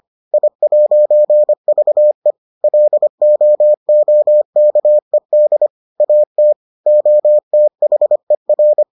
Morse code, heard as a keyed tone, about 25 wpm, 600 Hz. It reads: I'VE LOOKED AT OTHER